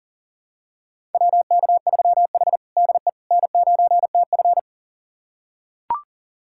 Morse code, heard as a keyed tone, about 40 wpm, 700 Hz. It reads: WK3H DE N9TF
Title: USA Callsign DE Callsign - 4 Characters - 40wpm